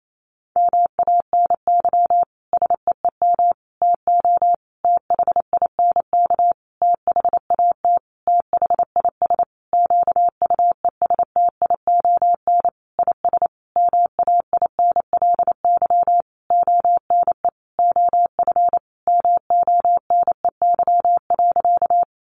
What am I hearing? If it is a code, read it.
Morse code, heard as a keyed tone, about 28 wpm, 700 Hz. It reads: MANY SEEM TO THINK THAT THIS QUESTION IS MAINLY ONE OF MONEY.
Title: Sentences from Top 1400 Words 28wpm